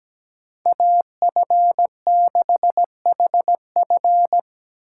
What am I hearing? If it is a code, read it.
Morse code, heard as a keyed tone, about 20 wpm, 700 Hz. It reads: AF6HF
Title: Speed Racing - USA 2x2 Call Signs 17wpm